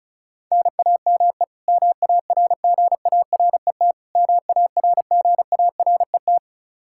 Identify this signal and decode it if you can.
Morse code, heard as a keyed tone, about 36 wpm, 700 Hz. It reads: NAME MARGARET MARGARET
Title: Practice QSO - Exchange 1 - Beginning - 35wpm